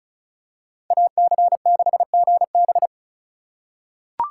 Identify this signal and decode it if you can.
Morse code, heard as a keyed tone, about 34 wpm, 700 Hz. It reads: AC6GB
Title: USA Call Signs - 2x Character Spacing - 35wpm